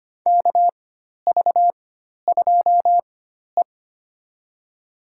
Morse code, heard as a keyed tone, about 25 wpm, spaced as for 12 wpm, 700 Hz. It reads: KV2E
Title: USA Call Signs - 4 Characters - 4x Character Spacing - 25wpm